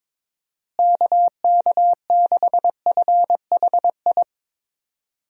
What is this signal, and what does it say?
Morse code, heard as a keyed tone, about 22 wpm, 700 Hz. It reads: KK6FHI
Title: USA Call Signs - 6 Characters - Repeated 2x - 22wpm